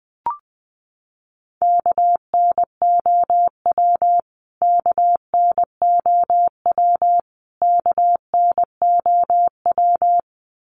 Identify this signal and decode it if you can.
Morse code, heard as a keyed tone, about 20 wpm, 700 Hz. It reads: KNOW KNOW KNOW
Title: Top 500 Words Repeated 20wpm